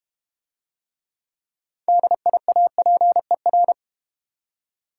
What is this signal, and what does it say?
Morse code, heard as a keyed tone, about 32 wpm, 700 Hz. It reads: DIAPER